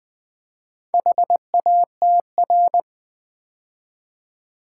Morse code, heard as a keyed tone, about 20 wpm, 700 Hz. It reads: HATR
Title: Build A Word - 2 Letters At A Time - 6-Letter Words - 20wpm